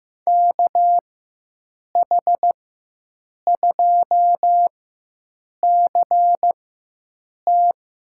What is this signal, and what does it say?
Morse code, heard as a keyed tone, about 15 wpm, spaced as for 7 wpm, 700 Hz. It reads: KH2CT
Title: USA Call Signs - 5 Characters - 4x Character Spacing - 15wpm